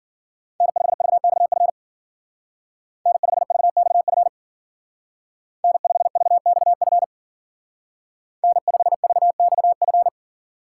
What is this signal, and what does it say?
Morse code, heard as a keyed tone, about 48 wpm, 700 Hz. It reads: N5VXF N5VXF N5VXF N5VXF
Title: Speed Racing - USA Call Signs 40wpm